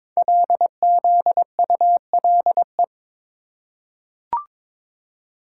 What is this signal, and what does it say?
Morse code, heard as a keyed tone, about 22 wpm, 700 Hz. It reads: LZULE
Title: Code Groups - 5 Letters - 22wpm